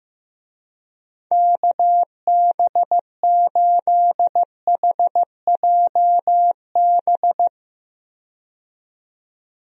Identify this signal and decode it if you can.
Morse code, heard as a keyed tone, about 15 wpm, 700 Hz. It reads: KB8HJB
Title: USA Call Signs - 6 Characters - 15wpm